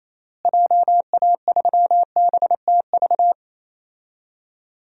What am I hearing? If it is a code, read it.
Morse code, heard as a keyed tone, about 28 wpm, 700 Hz. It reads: JA3BTV
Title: DXCC-100 Call Signs - 6 Characters - Repeated 3x - 28wpm